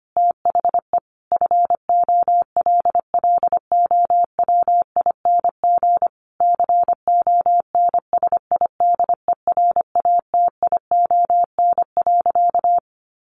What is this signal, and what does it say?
Morse code, heard as a keyed tone, about 25 wpm, 700 Hz. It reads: THE FOLLOWING CONSIDERATION.
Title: Sentences from Top 1500 Words 25wpm